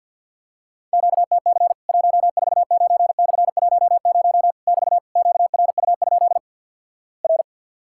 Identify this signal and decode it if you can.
Morse code, heard as a keyed tone, about 50 wpm, 700 Hz. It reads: QTC 149/10 <BT> QRU? R
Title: Worked All Europe DX Contest - QTC Exchange with Repeats - 50wpm